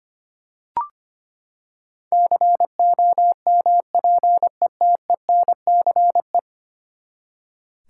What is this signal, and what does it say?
Morse code, heard as a keyed tone, about 25 wpm, 700 Hz. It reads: COMPETENCE
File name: practice / 25wpm / Ten Letter Words - 25wpm